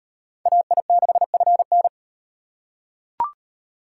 Morse code, heard as a keyed tone, about 35 wpm, 700 Hz. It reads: AI6FN